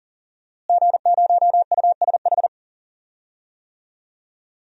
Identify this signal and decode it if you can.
Morse code, heard as a keyed tone, about 40 wpm, 700 Hz. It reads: G0USH
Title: DXCC-100 Call Signs - 5 Characters - Repeated 2x - 40wpm